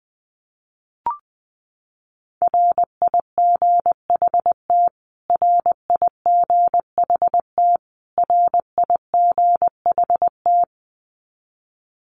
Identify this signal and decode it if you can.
Morse code, heard as a keyed tone, about 20 wpm, 700 Hz. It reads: RIGHT RIGHT RIGHT
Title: Top 300 Words Repeated 20wpm